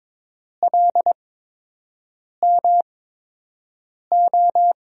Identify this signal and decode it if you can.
Morse code, heard as a keyed tone, about 21 wpm, spaced as for 6 wpm, 700 Hz. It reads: LMO
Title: USA Call Signs - 8x Character Spacing - 22wpm